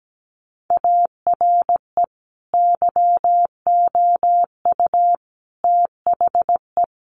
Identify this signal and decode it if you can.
Morse code, heard as a keyed tone, about 17 wpm, 700 Hz. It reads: ARE YOU THE
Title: Sentences from Top 1300 Words 17wpm